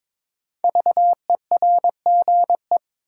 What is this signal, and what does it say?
Morse code, heard as a keyed tone, about 21 wpm, 700 Hz. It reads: VERGE
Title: Sets of 4 Words - New Word Drill - 22wpm